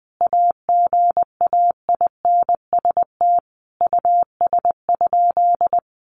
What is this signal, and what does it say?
Morse code, heard as a keyed tone, about 19 wpm, 700 Hz. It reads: AGAINST US?
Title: Sentences from Top 1500 Words 20wpm